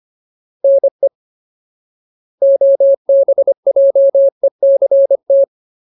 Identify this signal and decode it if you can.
Morse code, heard as a keyed tone, about 25 wpm, 550 Hz. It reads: NE OBJECT
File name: Ultimate Word List / 25WPM Ultimate